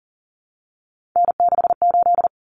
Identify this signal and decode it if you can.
Morse code, heard as a keyed tone, about 40 wpm, 700 Hz. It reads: N68
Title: Common Cut Numbers - 3-Digit 40wpm